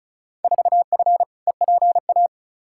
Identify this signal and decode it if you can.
Morse code, heard as a keyed tone, about 36 wpm, 700 Hz. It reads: 4F EPA
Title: Field Day Exchanges - 35wpm